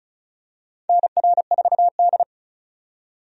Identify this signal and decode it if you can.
Morse code, heard as a keyed tone, about 36 wpm, 700 Hz. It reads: NR4D
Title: USA Call Signs - 4 Characters - 35wpm